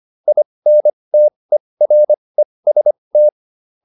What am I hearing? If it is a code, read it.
Morse code, heard as a keyed tone, about 25 wpm, spaced as for 20 wpm, 600 Hz. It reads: INTEREST
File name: IN8 20